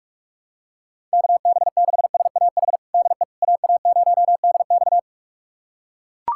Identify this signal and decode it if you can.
Morse code, heard as a keyed tone, about 42 wpm, 700 Hz. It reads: KB6SAH DE AA0DX
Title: USA Callsign DE Callsign 45wpm